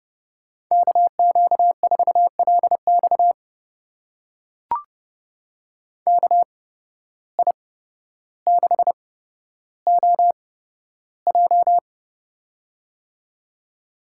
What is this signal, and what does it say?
Morse code, heard as a keyed tone, about 30 wpm, spaced as for 13 wpm, 700 Hz. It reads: KQ4LX KI6OJ
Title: USA Call Signs - 5 Characters - 8x Character Spacing - 30wpm